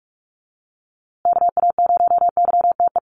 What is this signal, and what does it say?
Morse code, heard as a keyed tone, about 44 wpm, 700 Hz. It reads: KA0YTE
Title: USA Call Signs Repeated 2x 45wpm